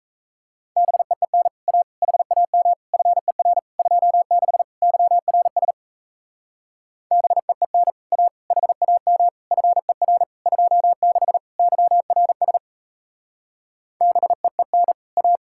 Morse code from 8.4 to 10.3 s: HAM FER

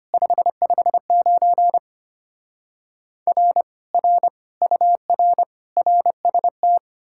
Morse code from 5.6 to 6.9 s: RST

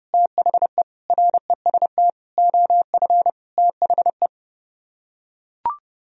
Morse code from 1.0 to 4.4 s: REST OF THE

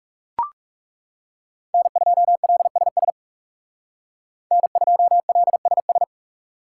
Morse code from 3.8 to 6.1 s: N2LSS

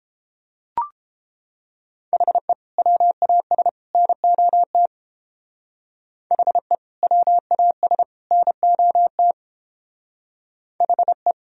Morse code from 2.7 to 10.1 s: WAS NOT HE WAS NOT

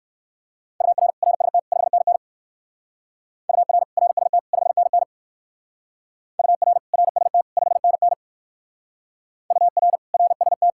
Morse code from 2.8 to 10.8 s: UR RST 5NN UR RST 5NN UR RST

